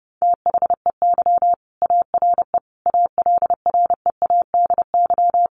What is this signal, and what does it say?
Morse code, heard as a keyed tone, about 30 wpm, 700 Hz. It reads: THEY ARE ALREADY
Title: Sentences from Top 1500 Words 30wpm